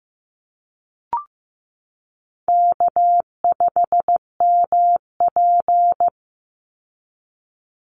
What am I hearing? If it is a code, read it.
Morse code, heard as a keyed tone, about 15 wpm, 700 Hz. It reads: K5MP
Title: Call Signs Repeated 3x - 15wpm